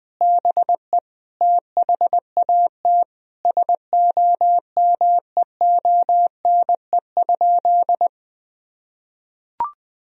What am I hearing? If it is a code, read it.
Morse code, heard as a keyed tone, about 20 wpm, 700 Hz. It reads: BE THAT SOMEONE?
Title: Sentences from Top 1000 Words - Review Part 4 - 20wpm